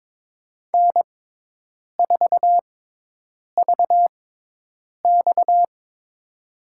Morse code, heard as a keyed tone, about 22 wpm, spaced as for 8 wpm, 700 Hz. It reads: N4VX
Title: USA Call Signs - 4 Characters - 6x Character Spacing - 22wpm